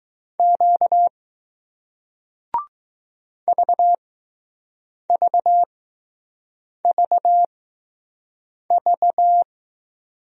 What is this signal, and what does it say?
Morse code, heard as a keyed tone, about 20 wpm, spaced as for 6 wpm, 700 Hz. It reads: Q VVVV